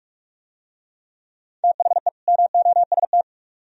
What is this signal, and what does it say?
Morse code, heard as a keyed tone, about 44 wpm, 700 Hz. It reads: THE MOST